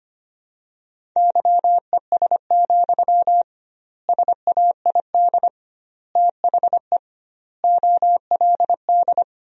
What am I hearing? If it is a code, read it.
Morse code, heard as a keyed tone, about 25 wpm, 700 Hz. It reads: YES, SAID THE OLD
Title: 2x Word-Spacing - Sentences from Top 500 Words - 25wpm